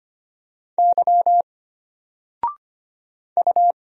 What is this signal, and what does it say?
Morse code, heard as a keyed tone, about 25 wpm, 700 Hz. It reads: Y U
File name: Single-Letter-Number-25wpm